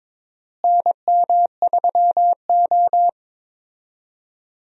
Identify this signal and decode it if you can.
Morse code, heard as a keyed tone, about 22 wpm, 700 Hz. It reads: NM3O
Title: USA Call Signs - 4 Characters - Repeated 3x - 22wpm